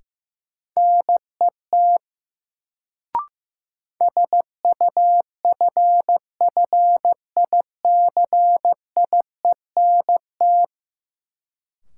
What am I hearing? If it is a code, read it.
Morse code, heard as a keyed tone, about 15 wpm, 700 Hz. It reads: NET SUFFICIENT
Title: Sentences from Top 1400 Words - New Word Drill - 15wpm